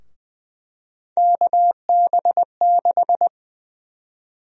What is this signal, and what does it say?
Morse code, heard as a keyed tone, about 20 wpm, 700 Hz. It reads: KB6
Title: USA Call Sign Prefixes - 20wpm